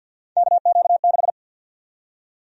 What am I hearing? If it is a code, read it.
Morse code, heard as a keyed tone, about 53 wpm, 700 Hz. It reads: KQ6
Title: USA Call Sign Prefixes - 50wpm